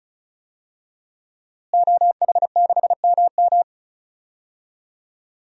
Morse code, heard as a keyed tone, about 34 wpm, 700 Hz. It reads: OH6MM
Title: DXCC-100 Call Signs - 5 Characters - Repeated 3x - 35wpm